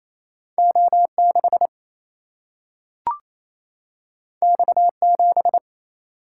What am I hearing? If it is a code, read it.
Morse code, heard as a keyed tone, about 28 wpm, 700 Hz. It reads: O 6 X 7
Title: Single-Letter Single-Number 28wpm